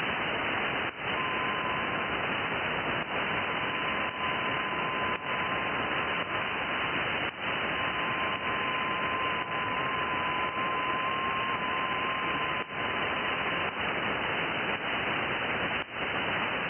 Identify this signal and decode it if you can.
FT8.
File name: IC-7100